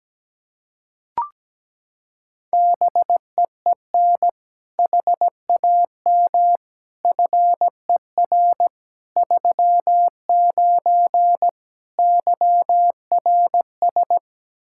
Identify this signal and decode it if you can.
Morse code, heard as a keyed tone, about 17 wpm, 700 Hz. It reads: BEEN HAM FER 39 YRS
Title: Practice QSO - Exchange 3 - More Info - 17wpm